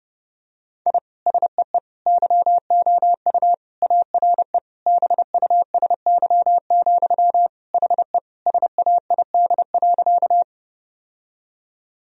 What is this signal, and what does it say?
Morse code, heard as a keyed tone, about 32 wpm, 700 Hz. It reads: I SEE YOU ARE BUSY, HE SAID.
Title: Sentences from Top 1000 Words - Review Part 3 - 30wpm